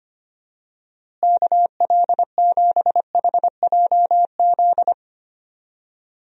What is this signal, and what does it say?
Morse code, heard as a keyed tone, about 25 wpm, 700 Hz. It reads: KL7HJZ
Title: DXCC-100 Call Signs - 6 Characters - Repeated 2x - 25wpm